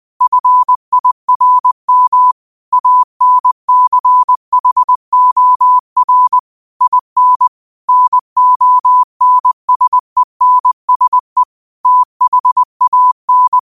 Morse code, heard as a keyed tone, about 20 wpm, 1000 Hz. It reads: FIRM ANCHOR IN NONSENSE THAN
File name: qotdfiles / 20wpm